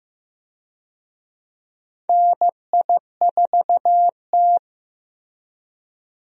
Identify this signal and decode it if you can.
Morse code, heard as a keyed tone, about 15 wpm, 700 Hz. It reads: NI4T
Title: USA Call Signs - 4 Characters - Repeated 2x - 15wpm